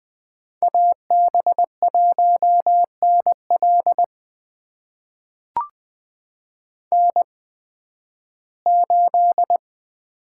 Morse code, heard as a keyed tone, about 20 wpm, spaced as for 13 wpm, 700 Hz. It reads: AB1NL N8
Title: USA Call Signs - 5 Characters - 8x Character Spacing - 20wpm